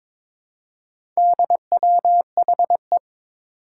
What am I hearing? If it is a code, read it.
Morse code, heard as a keyed tone, about 22 wpm, 700 Hz. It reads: DWHE
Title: Four Challenging Letter-Number Combinations 22wpm